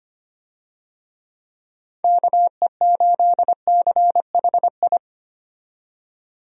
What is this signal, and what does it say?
Morse code, heard as a keyed tone, about 25 wpm, 700 Hz. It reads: KE8CHI
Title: USA Call Signs - 6 Characters - 25wpm